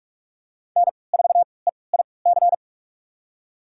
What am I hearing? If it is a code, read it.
Morse code, heard as a keyed tone, about 46 wpm, spaced as for 26 wpm, 700 Hz. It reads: N4EIC
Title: USA Call Signs - 5 Characters - 3x Character Spacing - 45wpm